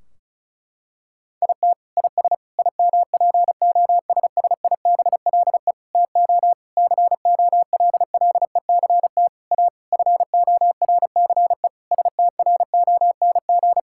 Morse code, heard as a keyed tone, about 36 wpm, 700 Hz. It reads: IT IS IMPOSSIBLE TO COLLECT A FORCE STRONG
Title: Sentences from Top 1600 Words 35wpm